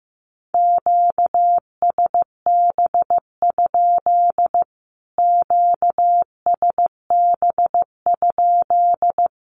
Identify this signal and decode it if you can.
Morse code, heard as a keyed tone, about 15 wpm, 700 Hz. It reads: QSB? QSB?
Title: Q-Codes Repeated 3x 15wpm